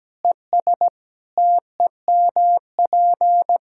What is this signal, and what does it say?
Morse code, heard as a keyed tone, about 17 wpm, 700 Hz. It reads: ES TEMP
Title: Practice QSO - Exchange 2 - Info - 17wpm